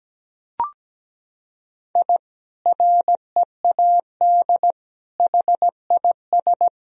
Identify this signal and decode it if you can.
Morse code, heard as a keyed tone, about 17 wpm, 700 Hz. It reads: I READ HIS
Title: Sentences from Top 1000 Words - Review Part 3 - 17wpm